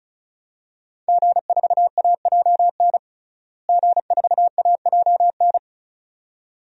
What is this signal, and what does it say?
Morse code, heard as a keyed tone, about 36 wpm, 700 Hz. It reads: G4AJN G4AJN
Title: DXCC-100 Call Signs - 5 Characters - Repeated 2x - 35wpm